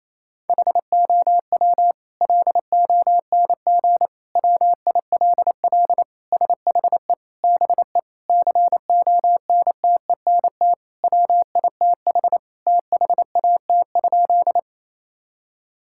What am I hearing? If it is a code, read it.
Morse code, heard as a keyed tone, about 28 wpm, 700 Hz. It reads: HOW LONG WILL SHE BE CONTENT WITH THAT?
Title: Sentences from Top 1000 Words - Review Part 6 - 28wpm